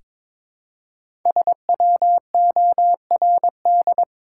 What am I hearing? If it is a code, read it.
Morse code, heard as a keyed tone, about 22 wpm, 700 Hz. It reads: SWORD
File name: 5 Letter Words - First 3 Letters Given - 22wpm